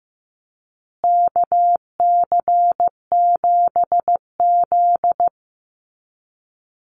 Morse code, heard as a keyed tone, about 14 wpm, 700 Hz. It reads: KC7Z
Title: USA Call Signs Repeated 3x 15wpm